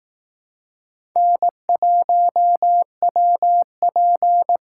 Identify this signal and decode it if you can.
Morse code, heard as a keyed tone, about 18 wpm, 700 Hz. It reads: N1WP
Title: Speed Racing - USA 1x2 Call Signs 15wpm